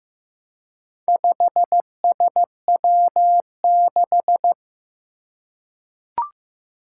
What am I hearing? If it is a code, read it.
Morse code, heard as a keyed tone, about 15 wpm, 700 Hz. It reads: 5SW6
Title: Four Challenging Letter-Number Combinations 15wpm